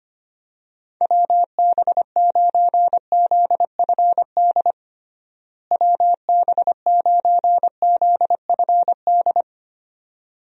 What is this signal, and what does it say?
Morse code, heard as a keyed tone, about 25 wpm, 700 Hz. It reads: WB9ZFD WB9ZFD
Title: USA Call Signs - 6 Characters - Repeated 2x - 25wpm